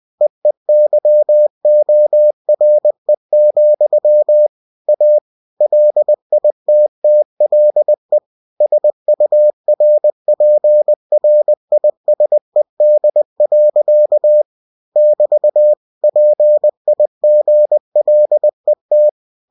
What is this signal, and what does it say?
Morse code, heard as a keyed tone, about 20 wpm, 600 Hz. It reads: EEYORE, A LITTLE SURPRISED. <BT> PIGLET
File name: cw-books / winnie-the-pooh / winnie-20wpm-page-0056